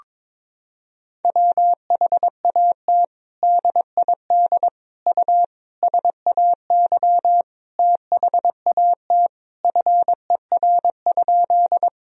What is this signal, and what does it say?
Morse code, heard as a keyed tone, about 22 wpm, 700 Hz. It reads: WHAT DID U SAY THAT FER?
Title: Sentences from Top 200 Words and Abbreviations 22wpm